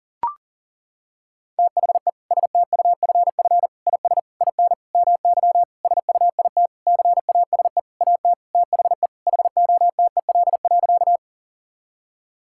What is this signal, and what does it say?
Morse code, heard as a keyed tone, about 40 wpm, 700 Hz. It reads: THE STUFF IS IN MY SUIT CASE AT THE HOTEL.
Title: Sentences from Top 1600 Words 40wpm